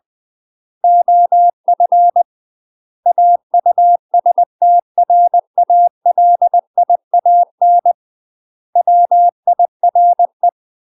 Morse code, heard as a keyed tone, about 20 wpm, 700 Hz. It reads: OF AUSTRALIAN WIRE
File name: UK MASTER QSOs TRAINING / 20WPM / UK MASTER QSOs 0026